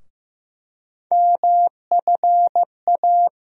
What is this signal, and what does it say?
Morse code, heard as a keyed tone, about 16 wpm, 700 Hz. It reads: MFA
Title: USA Call Sign Postfixes - 3 Letters 15wpm